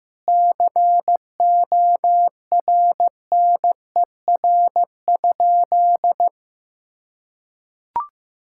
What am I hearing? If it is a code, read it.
Morse code, heard as a keyed tone, about 15 wpm, 700 Hz. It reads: CORNER?
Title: Sentences from Top 1600 Words 15wpm